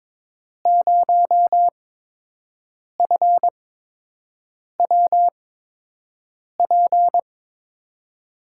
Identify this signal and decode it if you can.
Morse code, heard as a keyed tone, about 22 wpm, spaced as for 6 wpm, 700 Hz. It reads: 0FWP
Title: USA Call Signs - 6 Characters - 8x Character Spacing - 22wpm